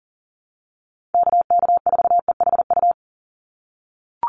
Morse code, heard as a keyed tone, about 40 wpm, 700 Hz. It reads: KK4EHU
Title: USA Call Signs 40wpm